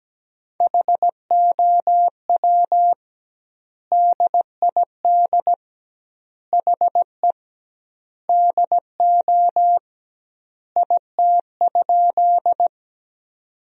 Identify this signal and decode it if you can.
Morse code, heard as a keyed tone, about 17 wpm, 700 Hz. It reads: HOW DID HE DO IT?
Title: 2x Word-Spacing - Sentences from Top 100 Words - 17wpm